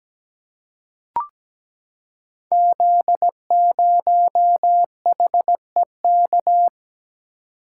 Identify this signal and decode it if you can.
Morse code, heard as a keyed tone, about 17 wpm, 700 Hz. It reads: Z0HEK
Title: Code Groups - 5 Letter:Numbers - 17wpm